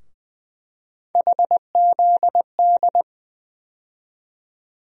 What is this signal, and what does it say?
Morse code, heard as a keyed tone, about 20 wpm, 700 Hz. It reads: HZD
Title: USA Call Sign Postfixes - 3 Letters 20wpm